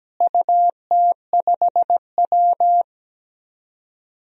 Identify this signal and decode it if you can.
Morse code, heard as a keyed tone, about 17 wpm, 700 Hz. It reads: UT5W
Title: DXCC-100 Call Signs - 4 Characters - Repeated 2x - 17wpm